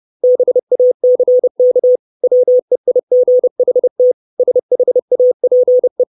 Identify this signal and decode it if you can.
Morse code, heard as a keyed tone, about 30 wpm, 500 Hz. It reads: BACK WEIGHT SHAPE
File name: 632 Words 30WPM